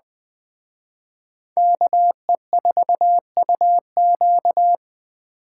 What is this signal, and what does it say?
Morse code, heard as a keyed tone, about 20 wpm, 700 Hz. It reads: KE4UQ
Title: Speed Racing - USA Call Signs 17wpm